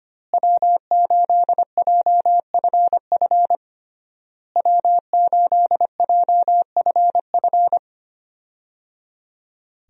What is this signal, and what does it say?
Morse code, heard as a keyed tone, about 25 wpm, 700 Hz. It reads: W8JFF W8JFF
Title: USA Call Signs - 5 Characters - Repeated 3x - 25wpm